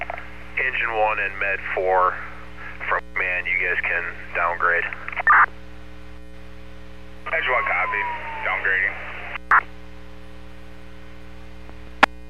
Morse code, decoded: none

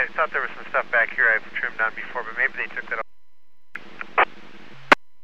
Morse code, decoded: none

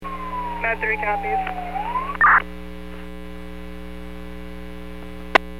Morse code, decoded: none

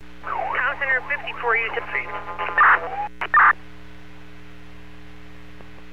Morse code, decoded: none

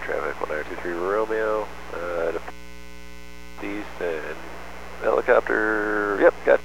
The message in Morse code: none

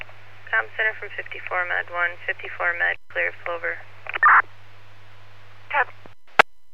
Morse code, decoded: none

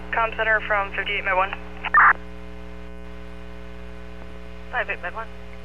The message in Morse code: none

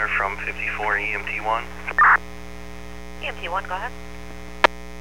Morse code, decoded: none

